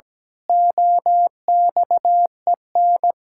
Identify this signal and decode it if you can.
Morse code, heard as a keyed tone, about 17 wpm, 700 Hz. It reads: OXEN
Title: Pangram Challenge v2 - 17wpm